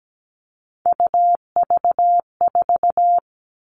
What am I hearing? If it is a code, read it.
Morse code, heard as a keyed tone, about 17 wpm, 700 Hz. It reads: UV4
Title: All Cut Numbers - 3-Digit 17wpm